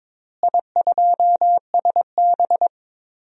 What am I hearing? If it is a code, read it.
Morse code, heard as a keyed tone, about 22 wpm, 700 Hz. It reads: I2SB
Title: DXCC-100 Call Signs - 4 Characters - Repeated 3x - 22wpm